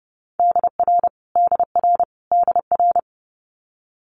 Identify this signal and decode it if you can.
Morse code, heard as a keyed tone, about 30 wpm, 700 Hz. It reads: DR DR DR
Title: Multi-letter Phonemes Repeated 3x 30wpm